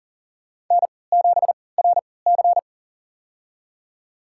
Morse code, heard as a keyed tone, about 40 wpm, spaced as for 23 wpm, 700 Hz. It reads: N7RC
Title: USA Call Signs - 4 Characters - 3x Character Spacing - 40wpm